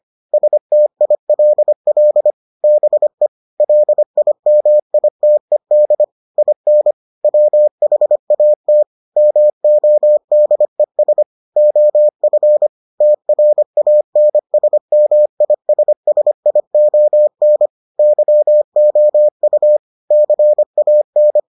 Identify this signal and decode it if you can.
Morse code, heard as a keyed tone, about 25 wpm, 600 Hz. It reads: STILL BE LIMITED IN WHAT MODES OF TRANSMISSION YOU CAN